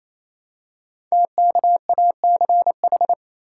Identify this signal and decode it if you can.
Morse code, heard as a keyed tone, about 29 wpm, 700 Hz. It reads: TKACH
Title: Code Groups - 5 Letters - 28wpm